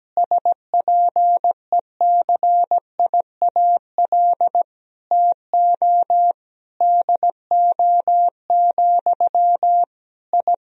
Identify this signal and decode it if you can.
Morse code, heard as a keyed tone, about 17 wpm, 700 Hz. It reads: SPECIAL TO DO, I
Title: Sentences from Top 1000 Words - Review Part 3 - 17wpm